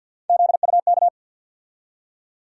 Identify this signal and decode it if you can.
Morse code, heard as a keyed tone, about 52 wpm, 700 Hz. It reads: ZUK
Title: USA Call Sign Postfixes - 3 Letters 50wpm